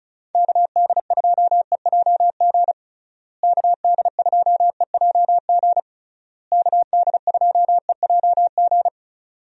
Morse code, heard as a keyed tone, about 34 wpm, 700 Hz. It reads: KD2EJG KD2EJG KD2EJG
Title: USA Call Signs - 6 Characters - Repeated 3x - 35wpm